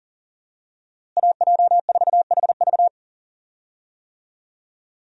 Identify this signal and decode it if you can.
Morse code, heard as a keyed tone, about 40 wpm, 700 Hz. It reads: AJ4HV
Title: USA Call Signs - 5 Characters - Repeated 3x - 40wpm